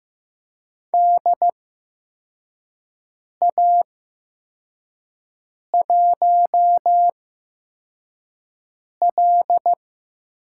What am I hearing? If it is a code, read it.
Morse code, heard as a keyed tone, about 15 wpm, spaced as for 4 wpm, 700 Hz. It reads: DA1L
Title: USA Call Signs - 8x Character Spacing - 15wpm